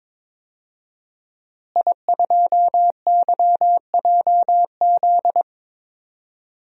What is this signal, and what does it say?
Morse code, heard as a keyed tone, about 22 wpm, 700 Hz. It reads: I2YJZ
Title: DXCC-100 Call Signs - 5 Characters - Repeated 3x - 22wpm